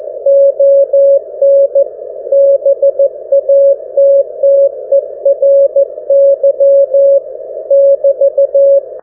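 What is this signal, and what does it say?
Morse code, heard as a keyed tone, about 15 wpm, 560 Hz. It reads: ON BATTERY <BT>
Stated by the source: GW4HXO 1W QRP Mini-QTX with B-Hive VFO, 472 kHz MF QSO with GW3UEP.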